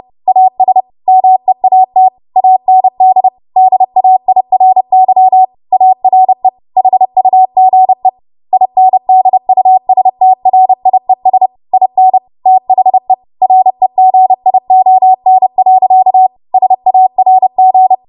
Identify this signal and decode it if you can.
Morse code, heard as a keyed tone, about 30 wpm, 750 Hz. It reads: AS MEAT AND DAIRY ARE HUGE INDUSTRIES IN THE REGION. SARG
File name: Morse / 30WPM